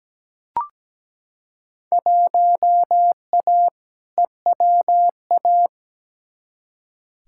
Morse code, heard as a keyed tone, about 17 wpm, 700 Hz. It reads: 1A EWA
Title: Field Day Exchanges Encore - 17wpm